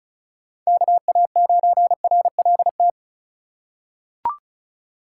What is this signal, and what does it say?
Morse code, heard as a keyed tone, about 36 wpm, 700 Hz. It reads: KA9RLT
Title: USA Call Signs - 6 Characters - 2x Character Spacing - 35wpm